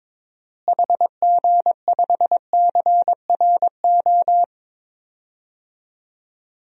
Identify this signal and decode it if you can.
Morse code, heard as a keyed tone, about 21 wpm, 700 Hz. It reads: HG5CRO
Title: DXCC-100 Call Signs - 6 Characters - Repeated 3x - 22wpm